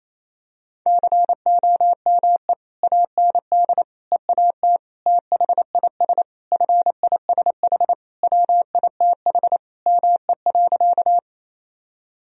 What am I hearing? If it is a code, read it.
Morse code, heard as a keyed tone, about 28 wpm, 700 Hz. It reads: COME AND EAT THIS FISH WITH ME.
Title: Sentences from Top 1000 Words - Review Part 1 - 28wpm